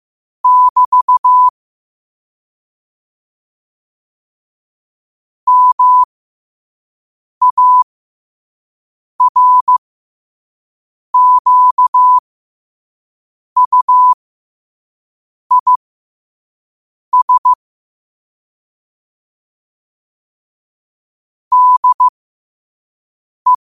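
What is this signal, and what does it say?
Morse code, heard as a keyed tone, about 15 wpm, spaced as for 5 wpm, 1000 Hz. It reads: <BT> MARQUIS DE
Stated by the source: 5 wpm Morse code practice recording.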